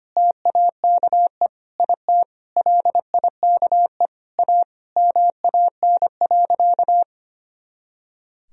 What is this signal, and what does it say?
Morse code, heard as a keyed tone, about 25 wpm, 700 Hz. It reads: TAKE IT LIKE A MAN.
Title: Sentences from Top 300 Words and Abbreviations 25wpm